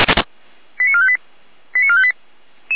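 Five-tone selective call (ZVEI), heard as quick runs of five tones, tones of about 80 ms. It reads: 89458 89458
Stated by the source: ZVEI five-tone selective call received on an SDR.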